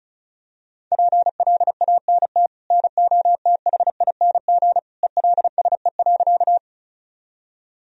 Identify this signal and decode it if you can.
Morse code, heard as a keyed tone, about 34 wpm, 700 Hz. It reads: PLANT NOTHING ELSE.
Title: Sentences from Top 1000 Words - Review Part 1 - 35wpm